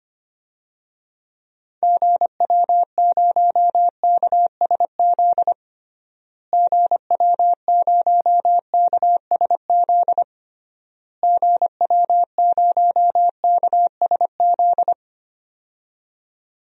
Morse code, heard as a keyed tone, about 25 wpm, 700 Hz. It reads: GW0KSZ GW0KSZ GW0KSZ
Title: DXCC-100 Call Signs - 6 Characters - Repeated 3x - 25wpm